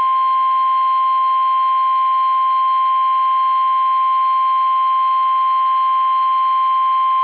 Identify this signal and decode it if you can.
Emergency Alert System (SAME) tones (the 1050 Hz attention tone). It is heard from 0.8 to 6.5 s.